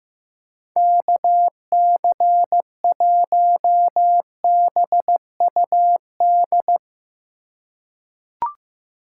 Morse code, heard as a keyed tone, about 15 wpm, 700 Hz. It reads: KC1BUD
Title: USA Call Signs - 6 Characters - 2x Character Spacing - 15wpm